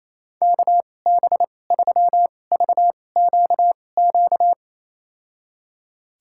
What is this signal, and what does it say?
Morse code, heard as a keyed tone, about 28 wpm, spaced as for 20 wpm, 700 Hz. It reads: KB3VQQ
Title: USA Call Signs - 6 Characters - 2x Character Spacing - 28wpm